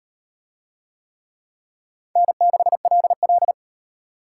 Morse code, heard as a keyed tone, about 36 wpm, 700 Hz. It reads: N6LL